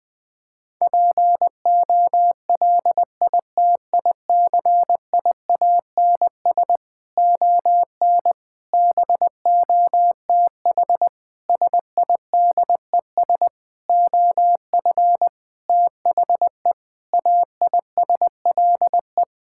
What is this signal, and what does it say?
Morse code, heard as a keyed tone, about 21 wpm, 700 Hz. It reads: POLITICIANS ON BOTH SIDES OF THE AISLE
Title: Sentences with 6-Word Phrases - Spoken Beginning - 20wpm